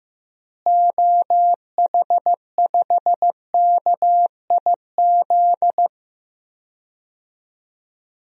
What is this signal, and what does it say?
Morse code, heard as a keyed tone, about 15 wpm, 700 Hz. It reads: OH5KIZ
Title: DXCC-100 Call Signs - 6 Characters - Repeated 3x - 15wpm